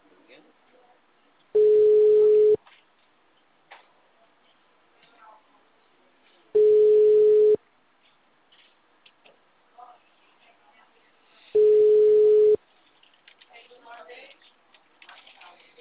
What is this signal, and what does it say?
Ringback tone (Europe, 425 Hz), with speech.